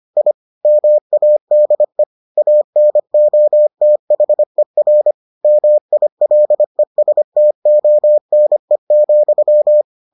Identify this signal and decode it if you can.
Morse code, heard as a keyed tone, about 25 wpm, 600 Hz. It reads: I MADE ANOTHER MILESTONE,